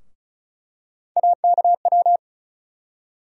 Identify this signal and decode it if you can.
Morse code, heard as a keyed tone, about 36 wpm, 700 Hz. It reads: AKW
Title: USA Call Sign Postfixes - 3 Letters 35wpm